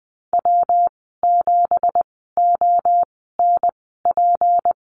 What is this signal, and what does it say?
Morse code, heard as a keyed tone, about 20 wpm, spaced as for 14 wpm, 700 Hz. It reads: W7ONP
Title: 179 Lesson 29g - Slash - Call Signs 20wpm